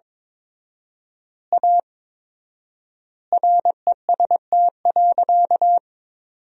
Morse code, heard as a keyed tone, about 22 wpm, 700 Hz. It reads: A REST.
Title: 4x Word-Spacing - Sentences from Top 500 Words - 22wpm